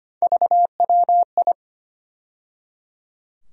Morse code, heard as a keyed tone, about 25 wpm, 700 Hz. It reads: VWI
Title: USA Call Sign Postfixes - 3 Letters 25wpm